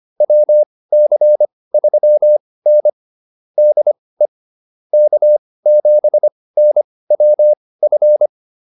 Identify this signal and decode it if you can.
Morse code, heard as a keyed tone, about 25 wpm, spaced as for 18 wpm, 600 Hz. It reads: WC3N DE K7NWF
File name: QSO209 18